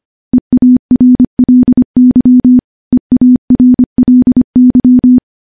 Morse code, heard as a keyed tone, about 25 wpm, 250 Hz. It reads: EARLY EARLY
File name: morse-training-25WPM